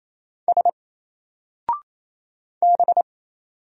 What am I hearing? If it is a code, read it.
Morse code, heard as a keyed tone, about 27 wpm, 700 Hz. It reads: S B
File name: Single-Letter-Number-28wpm